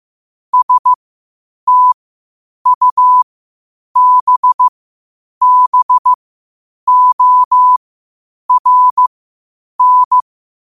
Morse code, heard as a keyed tone, about 15 wpm, spaced as for 8 wpm, 1000 Hz. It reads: STUBBORN
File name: qotdfiles / 8wpm